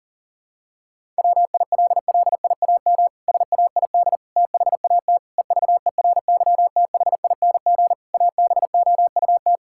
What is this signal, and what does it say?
Morse code, heard as a keyed tone, about 40 wpm, 700 Hz. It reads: WILLIAM SAID THAT EVERYTHING ABOUT
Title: Pangram Challenge v2 - 40wpm